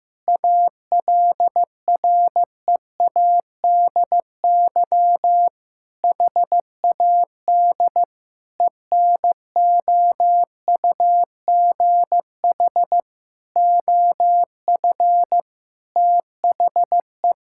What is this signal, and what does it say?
Morse code, heard as a keyed tone, about 15 wpm, 700 Hz. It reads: ALREADY HAD ENOUGH OF THE
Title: Sentences from Top 1600 Words 15wpm